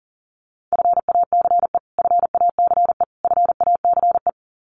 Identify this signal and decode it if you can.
Morse code, heard as a keyed tone, about 38 wpm, 700 Hz. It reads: FACE FACE FACE
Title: Top 500 Words Repeated 40wpm